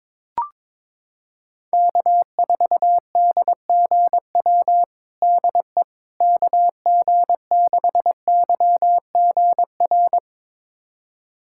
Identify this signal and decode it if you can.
Morse code, heard as a keyed tone, about 22 wpm, 700 Hz. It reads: K4DGW DE KG6YGR
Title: USA Callsign DE Callsign 22wpm